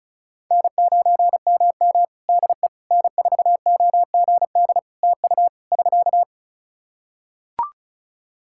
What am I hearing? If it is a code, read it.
Morse code, heard as a keyed tone, about 34 wpm, 700 Hz. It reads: N9MM DE N4OGD TU <SK>
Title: K1USN SST QSO Practice 35wpm